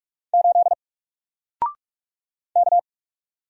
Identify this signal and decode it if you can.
Morse code, heard as a keyed tone, about 44 wpm, 700 Hz. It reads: 8 K
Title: Single Letter-Number 45wpm